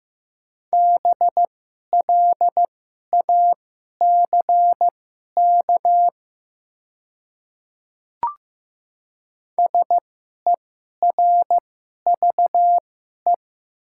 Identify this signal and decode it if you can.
Morse code, heard as a keyed tone, about 15 wpm, spaced as for 11 wpm, 700 Hz. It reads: BLACK SERVE
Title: 2x Character Spacing - Top 500 Words - 15wpm